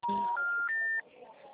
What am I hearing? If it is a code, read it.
Special information tone, with noise.